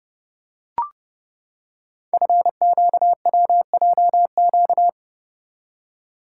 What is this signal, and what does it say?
Morse code, heard as a keyed tone, about 30 wpm, 700 Hz. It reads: FQWJQ
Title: Code Groups - 5 Letters - 30wpm